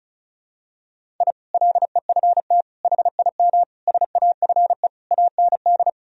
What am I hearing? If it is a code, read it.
Morse code, heard as a keyed tone, about 36 wpm, 700 Hz. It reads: I LEFT HIM SAFE AND
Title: Sentences from Top 1000 Words - Review Part 1 - 35wpm